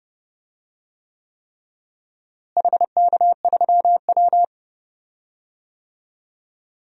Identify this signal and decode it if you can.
Morse code, heard as a keyed tone, about 30 wpm, 700 Hz. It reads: HK3W
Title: DXCC-100 Call Signs - 4 Characters - Repeated 3x - 30wpm